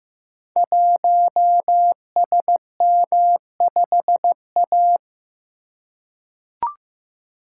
Morse code, heard as a keyed tone, about 15 wpm, 700 Hz. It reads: 1SM5A
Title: Code Groups - 5 Letter:Numbers - 15wpm